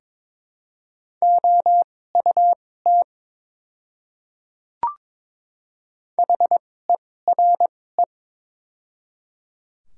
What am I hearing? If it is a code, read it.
Morse code, heard as a keyed tone, about 22 wpm, spaced as for 16 wpm, 700 Hz. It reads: OUT HERE
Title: 2x Character Spacing - Top 200 Words - 22wpm